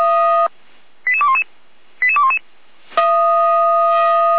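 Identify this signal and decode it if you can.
Five-tone selective call (ZVEI), heard as quick runs of five tones, tones of about 70 ms. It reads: 88211 88211
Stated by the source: ZVEI five-tone selective call 88419 received on SDR.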